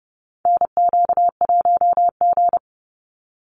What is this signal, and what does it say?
Morse code, heard as a keyed tone, about 30 wpm, 700 Hz. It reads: NQ1G
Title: USA Call Signs 30wpm